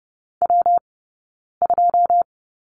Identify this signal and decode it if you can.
Morse code, heard as a keyed tone, about 30 wpm, spaced as for 9 wpm, 690 Hz. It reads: W2
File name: Lesson 19d - Letter w - Two Characters 30wpm